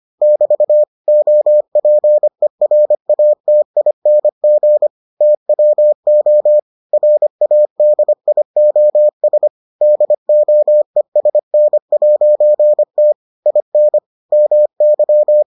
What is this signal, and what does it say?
Morse code, heard as a keyed tone, about 25 wpm, 600 Hz. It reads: <BT> OPERATING TWO RADIOS DOESN'T IN MY